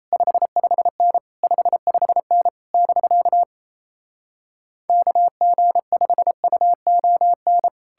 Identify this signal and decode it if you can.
Morse code, heard as a keyed tone, about 38 wpm, 700 Hz. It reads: 55N 55N <BK> KG5UON